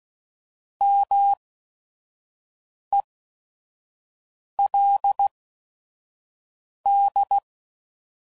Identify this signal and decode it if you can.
Morse code, heard as a keyed tone, about 16 wpm, spaced as for 5 wpm, 800 Hz. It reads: MELD